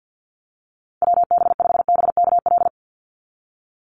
Morse code, heard as a keyed tone, about 50 wpm, 700 Hz. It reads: WB5BKL
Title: Call Signs - 50wpm